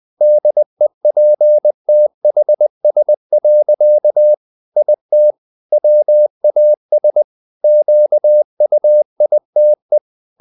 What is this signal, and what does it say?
Morse code, heard as a keyed tone, about 20 wpm, 600 Hz. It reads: DEPTHS. IT WAS QUITE
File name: the old man and the sea / the old man and the sea part 1 20wpm